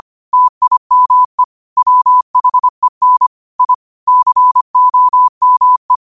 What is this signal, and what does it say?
Morse code, heard as a keyed tone, about 25 wpm, 1000 Hz. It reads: TIME WHEN I COME
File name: qotdfiles / 25wpm